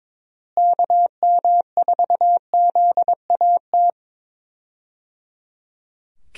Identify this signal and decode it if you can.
Morse code, heard as a keyed tone, about 22 wpm, 700 Hz. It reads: KM4ZAT
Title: USA Call Signs - 6 Characters - Repeated 3x - 22wpm